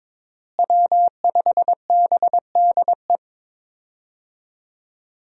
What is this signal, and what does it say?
Morse code, heard as a keyed tone, about 23 wpm, 700 Hz. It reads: W5BDE
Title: USA Call Signs - 5 Characters - Repeated 2x - 22wpm